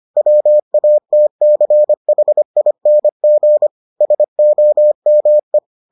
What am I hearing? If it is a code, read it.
Morse code, heard as a keyed tone, about 25 wpm, 600 Hz. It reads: WATCHING SOME